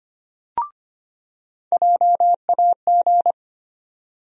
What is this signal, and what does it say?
Morse code, heard as a keyed tone, about 25 wpm, 700 Hz. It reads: JAG
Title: USA Call Sign Postfixes - 3 Letters 25wpm